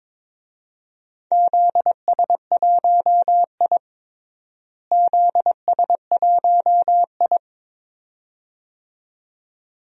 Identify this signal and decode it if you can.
Morse code, heard as a keyed tone, about 22 wpm, 700 Hz. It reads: ZS1I ZS1I
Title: DXCC-100 Call Signs - 4 Characters - Repeated 2x - 22wpm